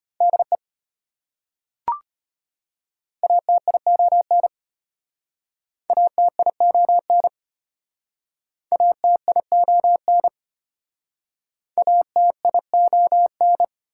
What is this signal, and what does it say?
Morse code, heard as a keyed tone, about 33 wpm, 700 Hz. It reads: DE ATION ATION ATION ATION